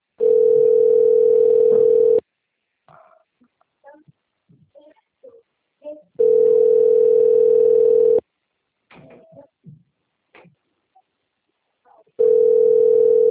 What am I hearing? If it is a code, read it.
Ringback tone (North America), 440 + 480 Hz.